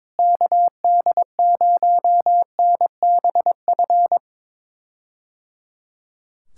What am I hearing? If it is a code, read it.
Morse code, heard as a keyed tone, about 21 wpm, 700 Hz. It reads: KD0NBF
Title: K1USN SST QSO Practice - 22wpm